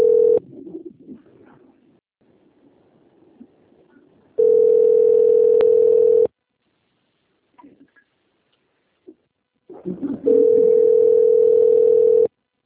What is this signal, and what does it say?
Ringback tone (North America), 440 + 480 Hz.
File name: out- - - - - .10551